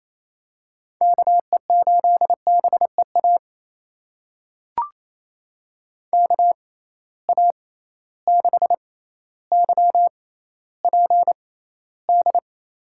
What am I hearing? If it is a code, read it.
Morse code, heard as a keyed tone, about 28 wpm, spaced as for 14 wpm, 700 Hz. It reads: KE8BEA KA6YPD
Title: USA Call Signs - 6 Characters - 6x Character Spacing - 28wpm